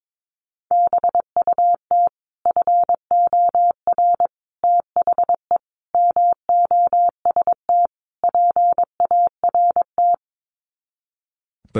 Morse code, heard as a keyed tone, about 22 wpm, 700 Hz. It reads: BUT FOR THE MOST PART
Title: Sets of 5 Words Encore 22wpm